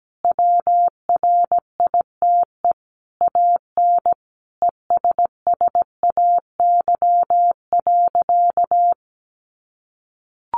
Morse code, heard as a keyed tone, about 17 wpm, 700 Hz. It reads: WRITE AN ESSAY.
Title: Sentences from Top 1500 Words 17wpm